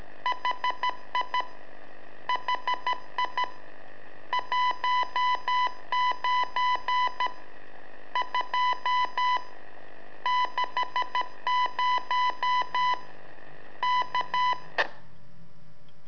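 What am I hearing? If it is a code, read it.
Morse code, heard as a keyed tone, about 13 wpm, 1000 Hz. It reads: HI HI 19 2 60 K
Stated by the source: Telemetry in Morse from LU7AA and LU4EZ, AMSAT and Turma Mar del Plata, sent from a balloon.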